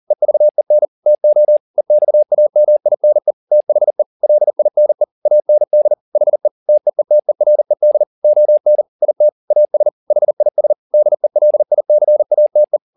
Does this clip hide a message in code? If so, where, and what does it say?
Yes: Morse code from 0.1 to 12.8 s, heard as a keyed tone, about 40 wpm, 600 Hz. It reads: EVEN TO EXAMINE THE LINE AND HE TEETERED ON IT AS HIS DELICATE